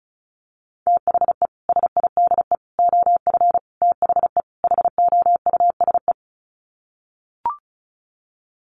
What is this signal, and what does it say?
Morse code, heard as a keyed tone, about 34 wpm, 700 Hz. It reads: THE SIDE OF THE HOUSE
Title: Sets of 5 Words Encore2 35wpm